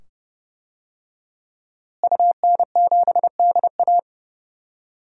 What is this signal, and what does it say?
Morse code, heard as a keyed tone, about 29 wpm, 700 Hz. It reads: UN7DA
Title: DXCC-100 Call Signs - 5 Characters - Repeated 2x - 30wpm